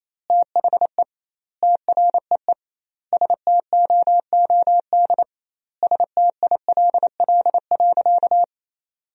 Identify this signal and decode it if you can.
Morse code, heard as a keyed tone, about 28 wpm, 700 Hz. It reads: THE TREE STOOD STILL.
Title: 2x Word-Spacing - Sentences from Top 500 Words - 28wpm